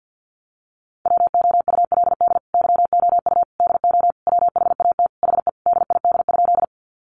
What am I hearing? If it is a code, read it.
Morse code, heard as a keyed tone, about 50 wpm, 700 Hz. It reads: WOULD YOU DO WHAT HE DID?
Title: Sentences from Top 100 Words 50wpm